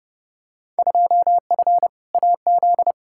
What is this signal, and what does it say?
Morse code, heard as a keyed tone, about 32 wpm, 700 Hz. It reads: 2F AZ
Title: Field Day Exchanges - 30wpm